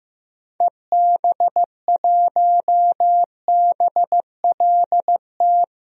Morse code, heard as a keyed tone, about 15 wpm, 700 Hz. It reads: EB1BLT
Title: DXCC-100 Call Signs - 6 Characters - Repeated 3x - 15wpm